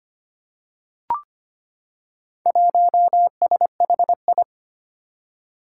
Morse code, heard as a keyed tone, about 25 wpm, 700 Hz. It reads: 1SHI
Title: Four Challenging Letter-Number Combinations 25wpm